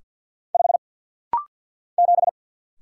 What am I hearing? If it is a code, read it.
Morse code, heard as a keyed tone, about 50 wpm, 690 Hz. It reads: 5 7
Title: Single Letter-Number - Rapid-Fire 50wpm